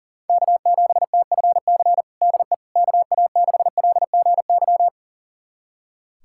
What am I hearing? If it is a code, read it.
Morse code, heard as a keyed tone, about 40 wpm, 700 Hz. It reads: K7TFC DE KA6LGY
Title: USA Callsign DE Callsign 40wpm